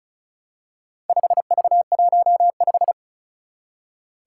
Morse code, heard as a keyed tone, about 34 wpm, 700 Hz. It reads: 5V15
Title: Four Challenging Letter-Number Combinations 35wpm